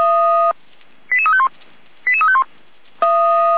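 Five-tone selective call (ZVEI), heard as quick runs of five tones, tones of about 70 ms. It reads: 88351 88351
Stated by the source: ZVEI five-tone selective call received on SDR.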